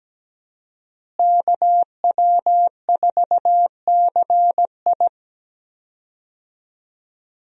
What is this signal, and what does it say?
Morse code, heard as a keyed tone, about 17 wpm, 700 Hz. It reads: KW4CI
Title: USA Call Signs - 5 Characters - Repeated 2x - 17wpm